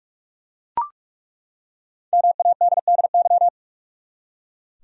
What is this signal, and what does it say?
Morse code, heard as a keyed tone, about 42 wpm, 700 Hz. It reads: MADDY
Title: USA Common Names - 45wpm